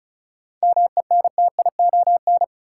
Morse code, heard as a keyed tone, about 36 wpm, 700 Hz. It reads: MENTION
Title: Seven Letter Words - 35wpm